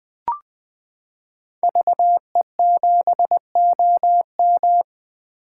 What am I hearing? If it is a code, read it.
Morse code, heard as a keyed tone, about 21 wpm, 700 Hz. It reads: VE7OM
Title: 2023 SOTA Chaser Call Signs - 20wpm